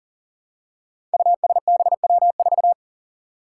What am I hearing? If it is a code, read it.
Morse code, heard as a keyed tone, about 40 wpm, 700 Hz. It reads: USBW4
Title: Five Challenging Letter-Number Combinations 40wpm